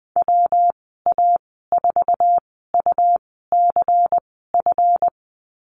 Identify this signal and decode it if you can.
Morse code, heard as a keyed tone, about 20 wpm, spaced as for 14 wpm, 700 Hz. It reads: WA4UCF
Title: 140 Lesson 23g - Letter f - Call Signs 20wpm